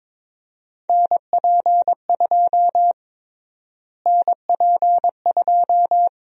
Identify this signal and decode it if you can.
Morse code, heard as a keyed tone, about 21 wpm, 700 Hz. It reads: NP2 NP2
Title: USA Call Sign Prefixes - Repeated 2x - 22wpm